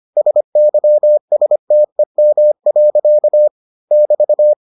Morse code, heard as a keyed tone, about 25 wpm, 600 Hz. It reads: SYSTEM. <BT>